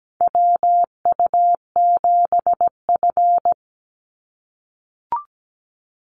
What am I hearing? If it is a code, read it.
Morse code, heard as a keyed tone, about 17 wpm, 700 Hz. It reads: WU7F
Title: Call Signs - 17wpm